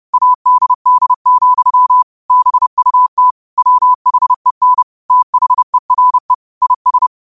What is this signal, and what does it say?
Morse code, heard as a keyed tone, about 30 wpm, 1000 Hz. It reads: ADD, BUT WHEN THERE IS